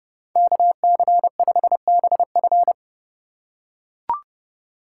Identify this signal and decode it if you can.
Morse code, heard as a keyed tone, about 29 wpm, 700 Hz. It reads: KC5BF